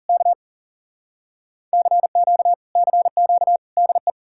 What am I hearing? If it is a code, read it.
Morse code, heard as a keyed tone, about 40 wpm, 700 Hz. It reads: K CQ CQ DE